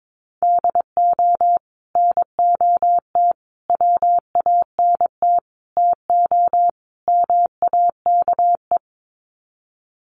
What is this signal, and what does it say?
Morse code, heard as a keyed tone, about 22 wpm, 700 Hz. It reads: DO NOT WANT TO MAKE
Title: Sets of 5 Words Encore 22wpm